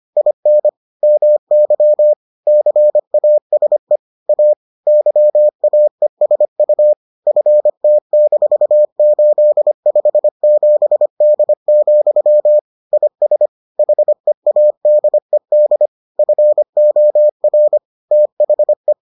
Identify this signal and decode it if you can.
Morse code, heard as a keyed tone, about 25 wpm, 600 Hz. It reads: IN MY CASE A YAESU FT-857D, IS HEADED FOR THE